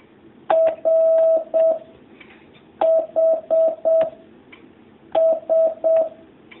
Morse code, heard as a keyed tone, about 7 wpm, 650 Hz. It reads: R H S
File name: jh8cbh / 2019cw